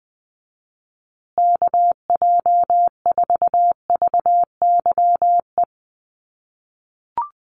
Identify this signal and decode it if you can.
Morse code, heard as a keyed tone, about 20 wpm, 700 Hz. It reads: KJ4VYE
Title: USA Call Signs Repeated 2x 20wpm